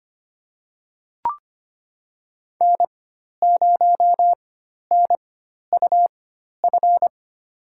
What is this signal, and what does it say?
Morse code, heard as a keyed tone, about 25 wpm, spaced as for 12 wpm, 700 Hz. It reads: N0NUF
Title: USA Call Signs - 4x Character Spacing - 25wpm